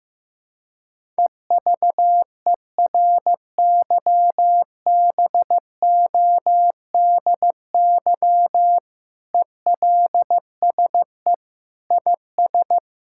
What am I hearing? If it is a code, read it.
Morse code, heard as a keyed tone, about 15 wpm, 700 Hz. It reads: EVERYBODY ELSE IS
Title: Sentences from Top 1600 Words 15wpm